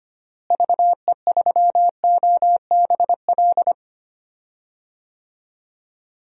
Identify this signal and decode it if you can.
Morse code, heard as a keyed tone, about 25 wpm, 700 Hz. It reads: VE3OBL
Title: DXCC-100 Call Signs - 6 Characters - Repeated 2x - 25wpm